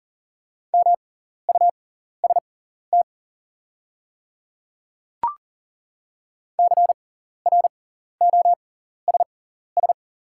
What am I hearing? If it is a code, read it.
Morse code, heard as a keyed tone, about 40 wpm, spaced as for 14 wpm, 690 Hz. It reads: MUST CROSS
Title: 6x Character Spacing - Top 500 Words - 40wpm